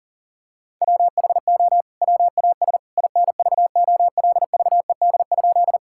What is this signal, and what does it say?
Morse code, heard as a keyed tone, about 40 wpm, 700 Hz. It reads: WHO WAS INVOLVED?
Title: Sentences from Top 1000 Words - Review Part 5 - 40wpm